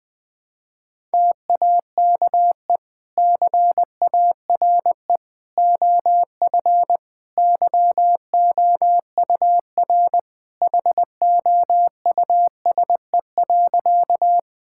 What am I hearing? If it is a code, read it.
Morse code, heard as a keyed tone, about 21 wpm, 700 Hz. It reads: TAKE CARE OF YOUR HOUSE.
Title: Sentences with 2-Word Phrases - 20wpm